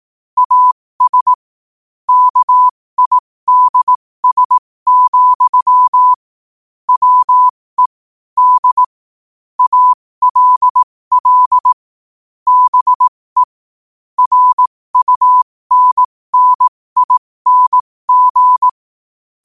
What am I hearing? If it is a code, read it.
Morse code, heard as a keyed tone, about 18 wpm, spaced as for 15 wpm, 1000 Hz. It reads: AS KIDS, WE D ALL BE RUNNING
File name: qotdfiles / 15wpm